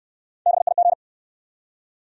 Morse code, heard as a keyed tone, about 68 wpm, 700 Hz. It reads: BIG